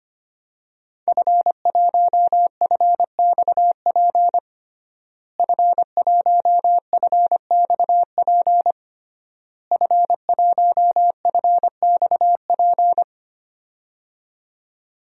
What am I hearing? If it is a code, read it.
Morse code, heard as a keyed tone, about 25 wpm, 700 Hz. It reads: F1FXP F1FXP F1FXP
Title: DXCC-100 Call Signs - 5 Characters - Repeated 3x - 25wpm